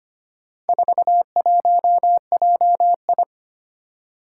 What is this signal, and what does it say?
Morse code, heard as a keyed tone, about 25 wpm, 700 Hz. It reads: 41JI
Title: Four Challenging Letter-Number Combinations 25wpm